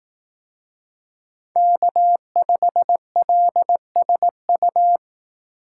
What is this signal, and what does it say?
Morse code, heard as a keyed tone, about 18 wpm, 700 Hz. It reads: K5LSU
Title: Speed Racing - USA 1x3 Call Signs 15wpm